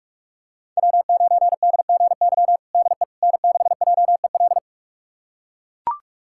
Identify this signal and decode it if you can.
Morse code, heard as a keyed tone, about 42 wpm, 700 Hz. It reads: W9DGY DE N6JEL
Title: USA Callsign DE Callsign - 5 Characters - 45wpm